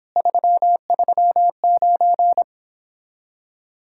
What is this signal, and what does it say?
Morse code, heard as a keyed tone, about 29 wpm, 700 Hz. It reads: 339